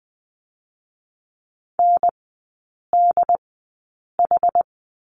Morse code, heard as a keyed tone, about 20 wpm, spaced as for 8 wpm, 700 Hz. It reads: NDH
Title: 067 Lesson 12e - Letter d - Three Characters 20wpm